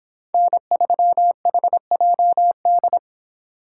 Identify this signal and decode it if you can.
Morse code, heard as a keyed tone, about 25 wpm, 700 Hz. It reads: N3HJD